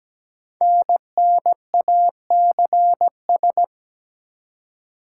Morse code, heard as a keyed tone, about 17 wpm, 700 Hz. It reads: NNACS
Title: Code Groups - 5 Letters - 17wpm